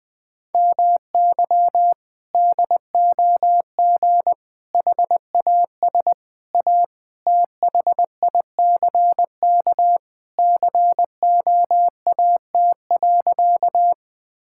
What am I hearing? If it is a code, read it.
Morse code, heard as a keyed tone, about 20 wpm, 700 Hz. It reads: MY DOG HAS A THICK COAT.
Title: Sentences from Top 1600 Words 20wpm